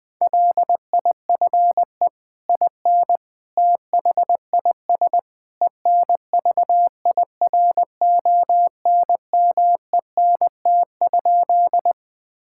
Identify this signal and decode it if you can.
Morse code, heard as a keyed tone, about 20 wpm, 700 Hz. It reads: LIFE IN THIS ENVIRONMENT?
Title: Sentences from Top 1000 Words - Review Part 5 - 20wpm